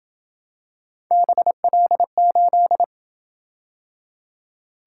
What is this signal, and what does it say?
Morse code, heard as a keyed tone, about 27 wpm, 700 Hz. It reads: BL8